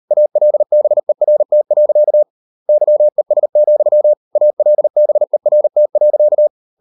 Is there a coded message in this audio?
Yes: Morse code from 0.1 to 6.5 s, heard as a keyed tone, about 38 wpm, 600 Hz. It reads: ALBERT. YES, ALBERT.